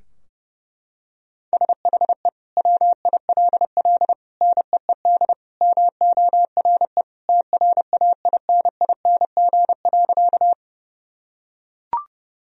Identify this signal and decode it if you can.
Morse code, heard as a keyed tone, about 30 wpm, 700 Hz. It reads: SHE WILL NEED MORE TRAINING.
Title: Sentences from Top 1000 Words - Review Part 4 - 30wpm